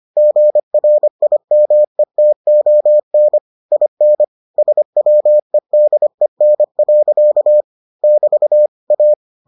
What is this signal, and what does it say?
Morse code, heard as a keyed tone, about 25 wpm, 600 Hz. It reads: GRIMETON IN SWEDEN. <BT> A